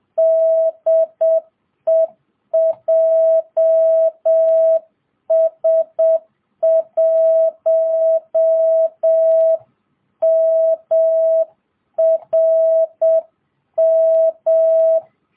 Morse code, heard as a keyed tone, about 7 wpm, 660 Hz. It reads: DE JS1MRM